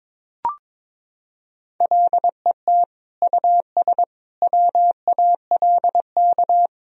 Morse code, heard as a keyed tone, about 21 wpm, 700 Hz. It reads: LET US WALK
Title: Sentences from Top 1000 Words - Review Part 3 - 22wpm